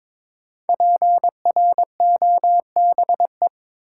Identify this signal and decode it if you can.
Morse code, heard as a keyed tone, about 23 wpm, 700 Hz. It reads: PROBE
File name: 5 Letter Words - First 3 Letters Given - 22wpm